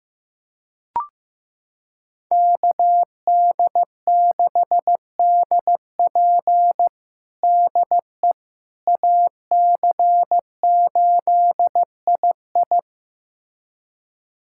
Morse code, heard as a keyed tone, about 15 wpm, 700 Hz. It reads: KD6DP DE AC8II
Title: USA Callsign DE Callsign - 5 Characters - 15wpm